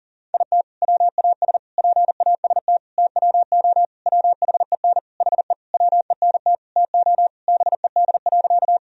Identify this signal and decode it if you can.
Morse code, heard as a keyed tone, about 40 wpm, 700 Hz. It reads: IT WAS PAST TWO WHEN HE WENT TO BED.
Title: Sentences from Top 1000 Words - Review Part 3 - 40wpm